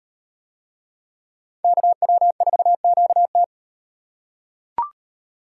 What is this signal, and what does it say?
Morse code, heard as a keyed tone, about 38 wpm, 700 Hz. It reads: KW4QT